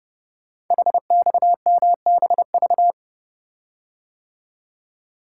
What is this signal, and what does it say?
Morse code, heard as a keyed tone, about 30 wpm, 690 Hz. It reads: HXMBV
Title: Code Groups - 5 Letters - 30wpm